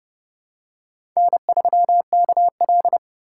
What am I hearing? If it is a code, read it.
Morse code, heard as a keyed tone, about 30 wpm, 700 Hz. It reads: N3KL
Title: USA Call Signs - 4 Characters - 3x Character Spacing - 30wpm